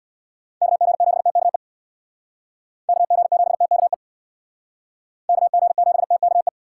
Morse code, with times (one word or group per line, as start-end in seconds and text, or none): 0.6-1.6 KG7TKE
2.9-3.9 KG7TKE
5.3-6.5 KG7TKE